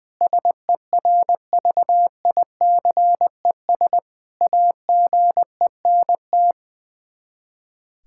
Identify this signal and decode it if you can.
Morse code, heard as a keyed tone, about 20 wpm, 700 Hz. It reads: SERVICES AGENT
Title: K1USN SST QSO Practice 20wpm